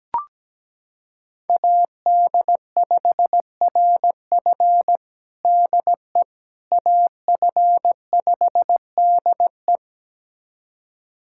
Morse code, heard as a keyed tone, about 17 wpm, 700 Hz. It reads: AD5RF DE AF5DE
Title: USA Callsign DE Callsign - 5 Characters - 17wpm